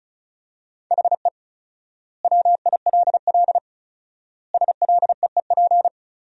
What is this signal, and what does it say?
Morse code, heard as a keyed tone, about 34 wpm, 700 Hz. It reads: HE WILL SLEEP
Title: 4x Word-Spacing - Sentences from Top 500 Words - 35wpm